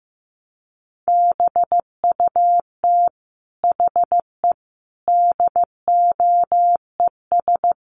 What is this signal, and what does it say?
Morse code, heard as a keyed tone, about 15 wpm, 700 Hz. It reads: BUT HE DOES
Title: Sentences from Top 200 Words 15wpm